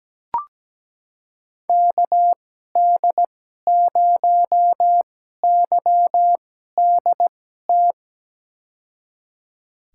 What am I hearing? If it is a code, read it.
Morse code, heard as a keyed tone, about 17 wpm, spaced as for 12 wpm, 700 Hz. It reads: KD0YDT
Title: USA Call Signs - 6 Characters - 2x Character Spacing - 17wpm